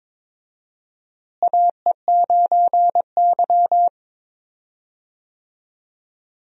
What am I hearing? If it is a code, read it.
Morse code, heard as a keyed tone, about 22 wpm, 700 Hz. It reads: AE9Y
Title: USA Call Signs - 4 Characters - Repeated 2x - 22wpm